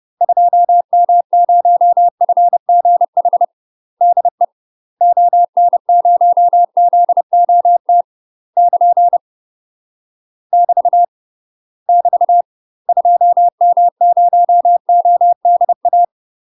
Morse code, heard as a keyed tone, about 30 wpm, 700 Hz. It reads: 2M0FGH DE ON0ZOT <KN> <BT> <BT> 2M0ODA
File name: UK MASTER QSOs TRAINING / 30WPM / UK MASTER QSOs 0044